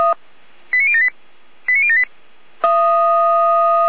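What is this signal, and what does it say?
Five-tone selective call (ZVEI), heard as quick runs of five tones, tones of about 70 ms. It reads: 89078 89078
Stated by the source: ZVEI five-tone selective call received on an SDR.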